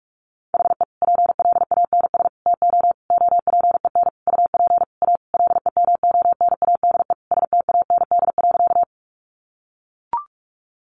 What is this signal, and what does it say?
Morse code, heard as a keyed tone, about 44 wpm, 700 Hz. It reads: HE PLANS TO OPEN UP A LEMONADE STAND.
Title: Additional Easy Phrases 45wpm